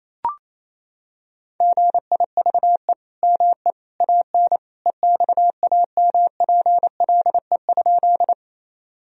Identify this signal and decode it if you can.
Morse code, heard as a keyed tone, about 28 wpm, 700 Hz. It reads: GIVE ME AN EXAMPLE?
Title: Sentences from Top 1000 Words - Review Part 1 - 28wpm